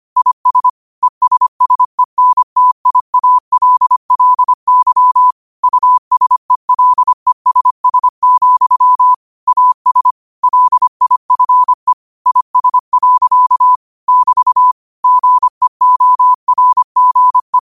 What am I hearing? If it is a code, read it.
Morse code, heard as a keyed tone, about 25 wpm, 1000 Hz. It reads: IS ESSENTIALLY USELESS, AS LIFE IS. <BT> GEORGE
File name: qotdfiles / 25wpm